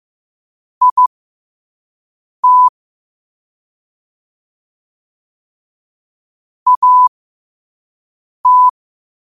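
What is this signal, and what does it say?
Morse code, heard as a keyed tone, about 15 wpm, spaced as for 5 wpm, 1000 Hz. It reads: IT AT
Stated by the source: Morse code quote of the day at 5 wpm.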